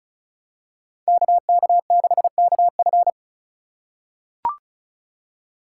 Morse code, heard as a keyed tone, about 36 wpm, 700 Hz. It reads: KK6KF
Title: USA Call Signs - 5 Characters - 4x Character Spacing - 35wpm